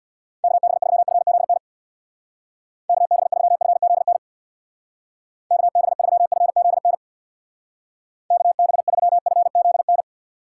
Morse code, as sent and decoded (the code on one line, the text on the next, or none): -.- -... ...-- ..-. --.. -. / -.- -... ...-- ..-. --.. -. / -.- -... ...-- ..-. --.. -. / -.- -... ...-- ..-. --.. -.
KB3FZN KB3FZN KB3FZN KB3FZN